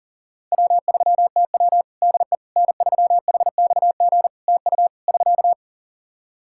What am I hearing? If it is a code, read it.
Morse code, heard as a keyed tone, about 40 wpm, 700 Hz. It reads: W3TW DE N3HXG TU <SK>
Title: Practice QSO - Exchange 4 - Ending - 40wpm